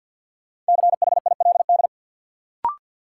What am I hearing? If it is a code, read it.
Morse code, heard as a keyed tone, about 53 wpm, 700 Hz. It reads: CHILD
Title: Sentences from Top 800 Words - New Word Drill - 50wpm